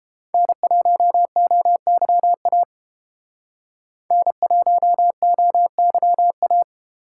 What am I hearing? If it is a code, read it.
Morse code, heard as a keyed tone, about 30 wpm, 700 Hz. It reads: N1OYA N1OYA